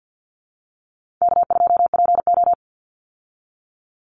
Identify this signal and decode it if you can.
Morse code, heard as a keyed tone, about 50 wpm, 700 Hz. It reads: K2PO
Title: Call Signs Repeated 2x - 50wpm